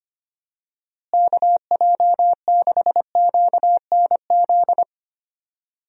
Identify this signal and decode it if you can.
Morse code, heard as a keyed tone, about 25 wpm, 700 Hz. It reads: KJ6QNZ
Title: USA Call Signs - 6 Characters - Repeated 2x - 25wpm